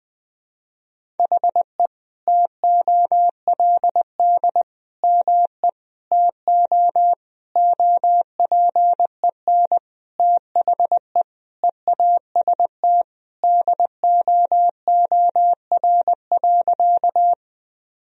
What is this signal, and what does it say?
Morse code, heard as a keyed tone, about 20 wpm, 700 Hz. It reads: HE TOLD ME TO OPEN THE EAST DOOR.
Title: Sentences from Top 1000 Words - Review Part 3 - 20wpm